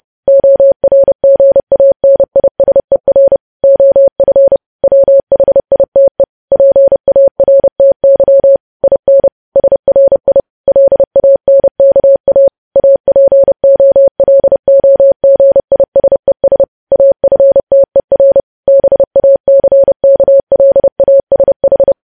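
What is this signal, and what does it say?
Morse code, heard as a keyed tone, about 30 wpm, 550 Hz. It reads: ORGANISER OF WHITE PARTY IN SRI LANKA APOLOGISES AFTER BACKLASH
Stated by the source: News headlines sent in Morse code at 30.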